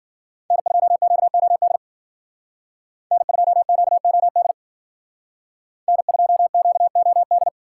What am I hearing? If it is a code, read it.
Morse code, heard as a keyed tone, about 48 wpm, 700 Hz. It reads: N2QOD N2QOD N2QOD
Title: Speed Racing - USA 1x3 Call Signs 40wpm